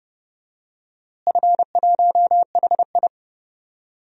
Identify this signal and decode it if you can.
Morse code, heard as a keyed tone, about 29 wpm, 700 Hz. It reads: F1HI
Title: DXCC-100 Call Signs - 4 Characters - Repeated 3x - 30wpm